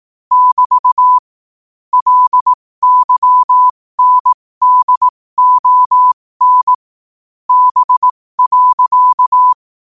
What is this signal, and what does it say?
Morse code, heard as a keyed tone, about 18 wpm, spaced as for 15 wpm, 1000 Hz. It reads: <BT> LYNDON B.
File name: qotdfiles / 15wpm